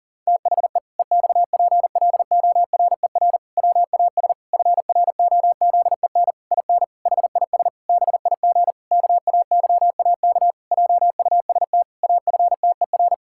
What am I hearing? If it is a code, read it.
Morse code, heard as a keyed tone, about 40 wpm, 700 Hz. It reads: THE EXPLORER WAS FROZEN IN HIS BIG KAYAK JUST AFTER
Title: Pangram Challenge v2 - 40wpm